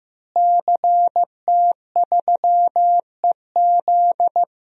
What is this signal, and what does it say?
Morse code, heard as a keyed tone, about 16 wpm, 700 Hz. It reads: CT3EZ
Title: DXCC-100 Call Signs - 5 Characters - Repeated 3x - 15wpm